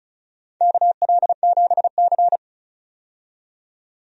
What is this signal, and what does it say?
Morse code, heard as a keyed tone, about 36 wpm, 700 Hz. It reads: KL7C
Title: DXCC-100 Call Signs - 4 Characters - 35wpm